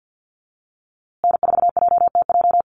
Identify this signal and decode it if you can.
Morse code, heard as a keyed tone, about 50 wpm, 700 Hz. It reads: N4JTJ
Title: USA Call Signs Repeated 3x 50wpm